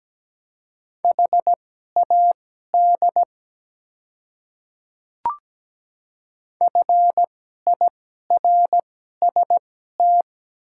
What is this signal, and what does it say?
Morse code, heard as a keyed tone, about 17 wpm, spaced as for 12 wpm, 700 Hz. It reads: HAD FIRST
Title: 2x Character Spacing - Top 300 Words - 17wpm